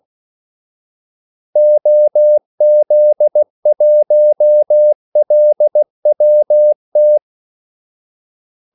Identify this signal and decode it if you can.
Morse code, heard as a keyed tone, about 16 wpm, 600 Hz. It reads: OZ1LWT
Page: Kaldesignaler 16 wpm